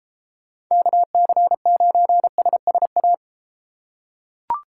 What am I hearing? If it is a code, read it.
Morse code, heard as a keyed tone, about 32 wpm, 700 Hz. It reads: KC9SSA